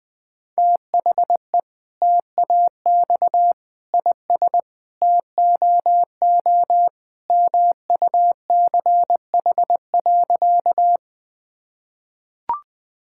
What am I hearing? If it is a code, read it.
Morse code, heard as a keyed tone, about 20 wpm, 700 Hz. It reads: THE TAX IS TOO MUCH.
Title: Sentences from Top 1000 Words - Review Part 5 - 20wpm